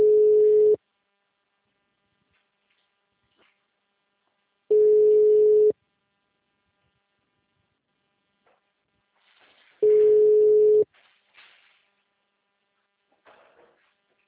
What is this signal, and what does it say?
Ringback tone (Europe, 425 Hz).